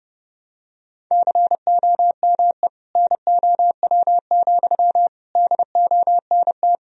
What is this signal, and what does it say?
Morse code, heard as a keyed tone, about 32 wpm, 700 Hz. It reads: COME NOW, DONT
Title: Sentences from Top 1000 Words - Review Part 2 - 30wpm